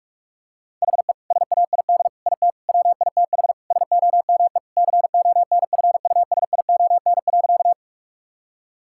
Morse code, heard as a keyed tone, about 46 wpm, 700 Hz. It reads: HE SAID IT WITH SOME CONFUSION.
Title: Sentences from Top 1600 Words 45wpm